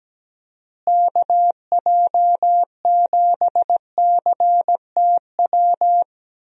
Morse code, heard as a keyed tone, about 17 wpm, 700 Hz. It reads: KJ7CTW
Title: USA Call Signs - 6 Characters - Repeated 3x - 17wpm